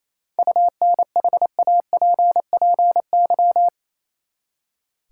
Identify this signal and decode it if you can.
Morse code, heard as a keyed tone, about 27 wpm, 700 Hz. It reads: UNHAPPY
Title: Seven Letter Words - 28wpm